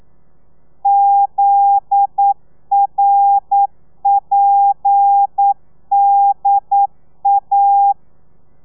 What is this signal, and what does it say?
Morse code, heard as a keyed tone, about 9 wpm, 800 Hz. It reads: ZRPDA